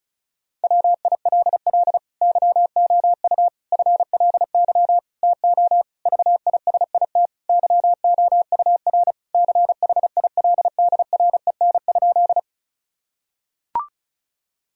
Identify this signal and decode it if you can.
Morse code, heard as a keyed tone, about 36 wpm, 700 Hz. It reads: WILL YOU FLY TO VISIT YOUR CHILDREN?
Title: Sentences from Top 1000 Words - Review Part 2 - 35wpm